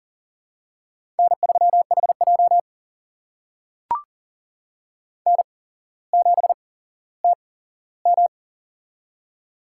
Morse code, heard as a keyed tone, about 40 wpm, spaced as for 17 wpm, 700 Hz. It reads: N3HJ N7TM
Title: USA Call Signs - 4 Characters - 8x Character Spacing - 40wpm